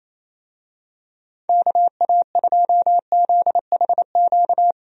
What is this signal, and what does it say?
Morse code, heard as a keyed tone, about 27 wpm, 700 Hz. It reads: KA2ZHQ
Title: USA Call Signs - 6 Characters - 28wpm